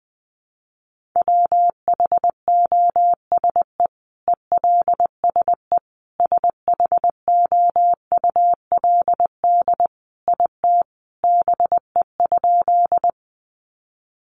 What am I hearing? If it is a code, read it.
Morse code, heard as a keyed tone, about 20 wpm, 700 Hz. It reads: WHOSE ELSE SHOULD IT BE?
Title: Sentences from Top 1200 Words 20wpm